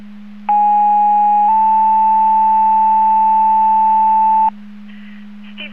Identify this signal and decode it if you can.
Two-tone page, heard as a one-second tone, then a three-second tone, about 830 then 880 Hz. It is heard from 0.5 to 4.5 s.